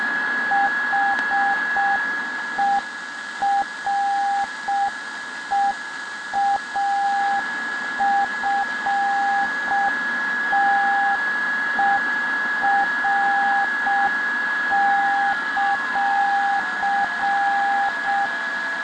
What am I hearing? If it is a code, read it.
Morse code, heard as a keyed tone, about 6 wpm, 800 Hz, with noise. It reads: HEREAFTER;